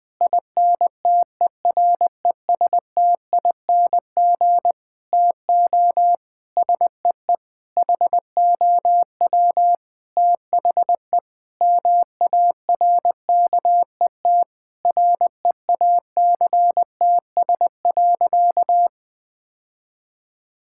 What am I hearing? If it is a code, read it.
Morse code, heard as a keyed tone, about 21 wpm, 700 Hz. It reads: INTERESTING TO SEE HOW THE MARKET REACTS.
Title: Sentences with 8-Word Phrases - 20wpm